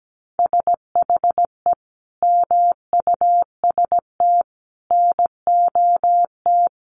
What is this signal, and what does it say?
Morse code, heard as a keyed tone, about 17 wpm, 700 Hz. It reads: SHE MUST NOT
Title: Sentences from Top 1500 Words 17wpm